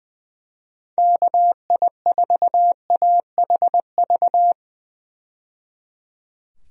Morse code, heard as a keyed tone, about 20 wpm, 700 Hz. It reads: KI4AHV
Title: USA Call Signs - 6 Characters - 20wpm